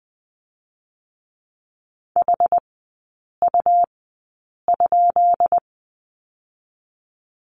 Morse code, heard as a keyed tone, about 21 wpm, spaced as for 8 wpm, 700 Hz. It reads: HU?
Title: 138 Lesson 23e - Letter f - Three Characters 20wpm